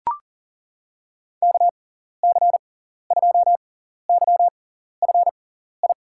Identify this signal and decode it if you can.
Morse code, heard as a keyed tone, about 40 wpm, spaced as for 14 wpm, 700 Hz. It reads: KC2YFI
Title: USA Call Signs - 6 Characters - 6x Character Spacing - 40wpm